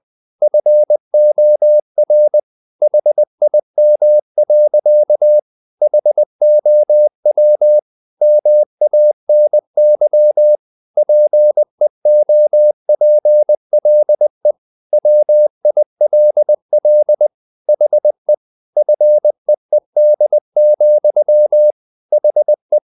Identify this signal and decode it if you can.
Morse code, heard as a keyed tone, about 20 wpm, 600 Hz. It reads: FOR HIM. HOW MANY PEOPLE WILL HE FEED, HE
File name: the old man and the sea part 2 20wpm